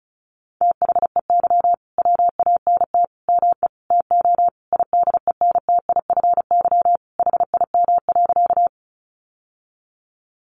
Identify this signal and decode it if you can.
Morse code, heard as a keyed tone, about 34 wpm, 700 Hz. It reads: THEY WANT ME TO IDENTIFY HIM.
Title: Sentences from Top 900 Words 35wpm